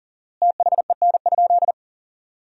Morse code, heard as a keyed tone, about 40 wpm, 700 Hz. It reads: THEN?
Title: 4x Word-Spacing - Sentences from Top 100 Words - 40wpm